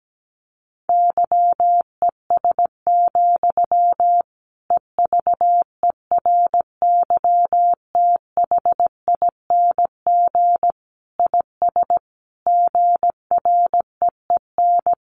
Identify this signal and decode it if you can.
Morse code, heard as a keyed tone, about 17 wpm, 700 Hz. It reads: YES, EVERYTHING IS GREEN
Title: Sentences from Top 700 Words 17wpm